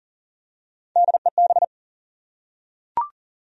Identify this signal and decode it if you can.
Morse code, heard as a keyed tone, about 40 wpm, 700 Hz. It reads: DEB
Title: USA Common Names - Max 3 Letters - 40wpm